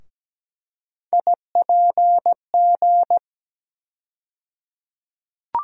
Morse code, heard as a keyed tone, about 17 wpm, 700 Hz. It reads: IPG
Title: USA Call Sign Postfixes - 3 Letters 17wpm